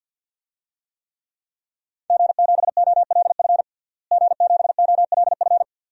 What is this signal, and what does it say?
Morse code, heard as a keyed tone, about 51 wpm, 700 Hz. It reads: G7OLF G7OLF
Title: DXCC-100 Call Signs - 5 Characters - Repeated 2x - 50wpm